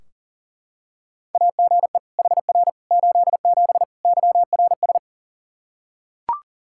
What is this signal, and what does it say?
Morse code, heard as a keyed tone, about 40 wpm, 700 Hz. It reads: AGE HR 87 YRS
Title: Practice QSO - Exchange 3 - More Info - 40wpm